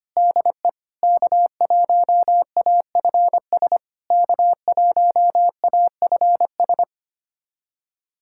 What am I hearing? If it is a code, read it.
Morse code, heard as a keyed tone, about 25 wpm, 690 Hz. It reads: DE K1AFS K1AFS
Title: SOTA QSO Practice 25wpm